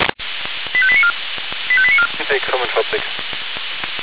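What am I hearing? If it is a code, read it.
Five-tone selective call (ZVEI), heard as quick runs of five tones, tones of about 70 ms, with speech. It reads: 85894 85894
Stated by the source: ZVEI five-tone selective call received on SDR.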